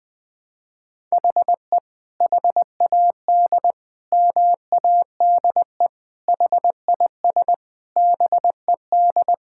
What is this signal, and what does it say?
Morse code, heard as a keyed tone, about 21 wpm, 700 Hz. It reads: HE HAD MADE HIS BED
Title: Sentences from Top 1600 Words 20wpm